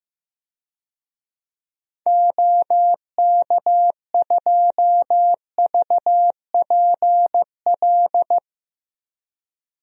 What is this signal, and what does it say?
Morse code, heard as a keyed tone, about 15 wpm, 700 Hz. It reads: OK2VPL
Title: DXCC-100 Call Signs - 6 Characters - Repeated 2x - 15wpm